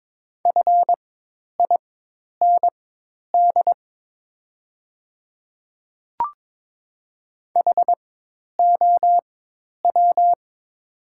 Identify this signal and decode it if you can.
Morse code, heard as a keyed tone, about 22 wpm, spaced as for 10 wpm, 700 Hz. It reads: FIND HOW
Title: 4x Character Spacing - Top 100 Words - 22wpm